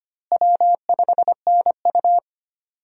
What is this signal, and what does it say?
Morse code, heard as a keyed tone, about 25 wpm, 700 Hz. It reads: W5NU
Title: USA Call Signs - 4 Characters - Repeated 2x - 25wpm